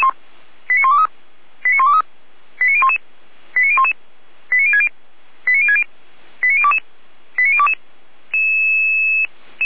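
Five-tone selective call (ZVEI), heard as quick runs of five tones, tones of about 70 ms. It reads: 89123 89123 89011 89011 89060 89060 89022 89022
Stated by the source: ZVEI five-tone call 89163 received on SDR.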